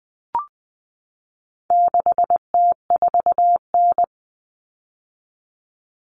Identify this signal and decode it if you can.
Morse code, heard as a keyed tone, about 20 wpm, 700 Hz. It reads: 6T4N
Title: Common Cut Numbers - 4-Digit 20wpm